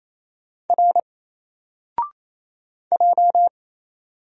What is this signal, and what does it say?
Morse code, heard as a keyed tone, about 27 wpm, 700 Hz. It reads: R J
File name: Single-Letter-Number-28wpm